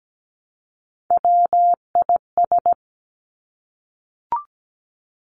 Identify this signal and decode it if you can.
Morse code, heard as a keyed tone, about 17 wpm, 700 Hz. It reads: WIS
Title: Three Challenging Letter-Number Combinations 17wpm